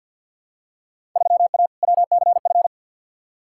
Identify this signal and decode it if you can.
Morse code, heard as a keyed tone, about 49 wpm, 700 Hz. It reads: 3A WCF
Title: Field Day Exchanges Encore - 50wpm